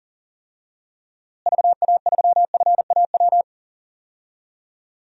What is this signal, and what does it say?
Morse code, heard as a keyed tone, about 37 wpm, 700 Hz. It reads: VA3FAW
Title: DXCC-100 Call Signs - 6 Characters - Repeated 3x - 40wpm